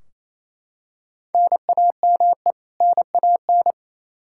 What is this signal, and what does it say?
Morse code, heard as a keyed tone, about 28 wpm, 700 Hz. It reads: NAME NAN
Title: Practice QSO - Exchange 1 - Beginning - 28wpm